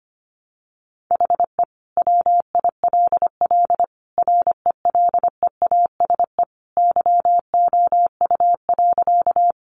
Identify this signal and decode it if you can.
Morse code, heard as a keyed tone, about 25 wpm, 700 Hz. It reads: HE WILL RELEASE YOU.
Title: Sentences from Top 1100 Words 25wpm